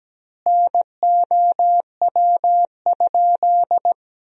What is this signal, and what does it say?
Morse code, heard as a keyed tone, about 17 wpm, 700 Hz. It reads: NOW?
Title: 3x Word-Spacing - Sentences from Top 400 Words - 17wpm